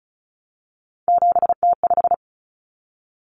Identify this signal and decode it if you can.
Morse code, heard as a keyed tone, about 34 wpm, 700 Hz. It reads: 7T5
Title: Common Cut Numbers - 3-Digit 35wpm